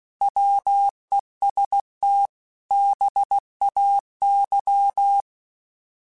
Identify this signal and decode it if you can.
Morse code, heard as a keyed tone, about 16 wpm, 800 Hz. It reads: WESTBAY
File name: fromSMs / FromSMs mp3s / Cities@15